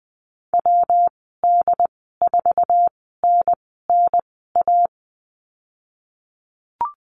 Morse code, heard as a keyed tone, about 20 wpm, spaced as for 14 wpm, 700 Hz. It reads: WD4NNA
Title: 122 Lesson 20g - Number 3 - Call Signs 20wpm